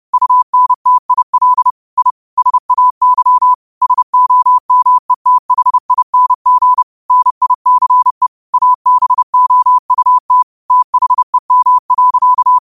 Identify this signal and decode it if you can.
Morse code, heard as a keyed tone, about 30 wpm, 1000 Hz. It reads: UNTIL I SAY SOMETHING NICE ABOUT THEM.
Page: Quotes for Fri, 15 Aug 2025 in Morse Code at 30 words per minute.